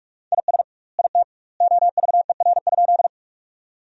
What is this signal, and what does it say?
Morse code, heard as a keyed tone, about 46 wpm, 700 Hz. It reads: IS IT OVER?
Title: 2x Word-Spacing - Sentences from Top 100 Words - 45wpm